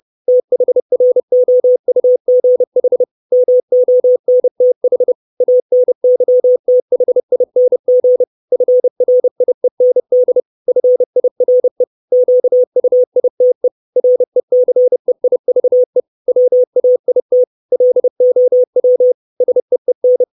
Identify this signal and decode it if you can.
Morse code, heard as a keyed tone, about 30 wpm, 500 Hz. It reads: THROUGH MONTH ANYTHING FRIEND FIRE QUITE RECEIVE WAIT LOW SEEN
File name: 632 Words 30WPM